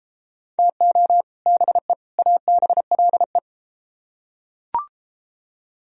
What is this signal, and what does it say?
Morse code, heard as a keyed tone, about 32 wpm, 700 Hz. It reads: TO BE ABLE